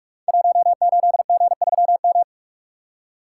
Morse code, heard as a keyed tone, about 46 wpm, 700 Hz. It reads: 18G3M
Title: Code Groups - 5 Letter:Numbers - 45wpm